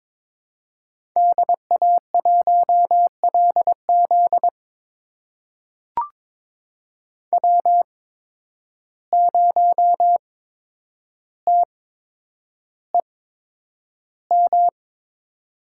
Morse code, heard as a keyed tone, about 22 wpm, spaced as for 9 wpm, 700 Hz. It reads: DA1LZ W0TEM
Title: USA Call Signs - 8x Character Spacing - 22wpm